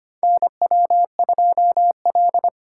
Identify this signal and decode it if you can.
Morse code, heard as a keyed tone, about 25 wpm, 700 Hz. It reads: NW2L
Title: USA Call Signs - 2x Character Spacing - 25wpm